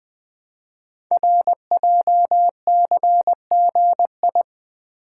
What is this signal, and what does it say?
Morse code, heard as a keyed tone, about 20 wpm, 700 Hz. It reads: RJCGI
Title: Code Groups - 5 Letters - 20wpm